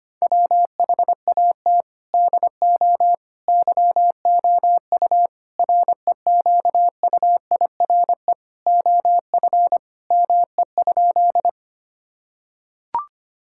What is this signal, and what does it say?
Morse code, heard as a keyed tone, about 25 wpm, 700 Hz. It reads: WHAT DO YOU REQUIRE OF ME?
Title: Sentences from Top 1000 Words - Review Part 6 - 25wpm